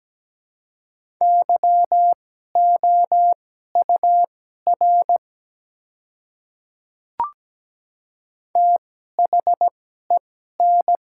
Morse code, heard as a keyed tone, about 17 wpm, spaced as for 12 wpm, 700 Hz. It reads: YOUR THEN
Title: 2x Character Spacing - Top 100 Words - 17wpm